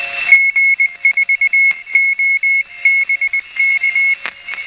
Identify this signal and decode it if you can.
Morse code, heard as a keyed tone, about 20 wpm, 2200 Hz. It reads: G4OBK
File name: attachments / DSB RX